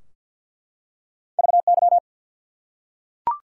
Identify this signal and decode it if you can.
Morse code, heard as a keyed tone, about 52 wpm, 700 Hz. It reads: VY
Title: Top 90 QSO Elements 50wpm